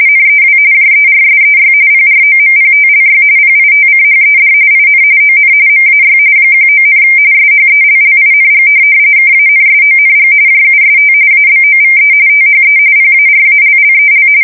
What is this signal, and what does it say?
RTTY, about 45.45 baud. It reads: NEW JERSEY PROMPTLY PITCHED IN ON MAY 19, SHELLING KANSONG JUST ABOVE THE PARALLEL
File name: fsk170 / USS-New-Jersey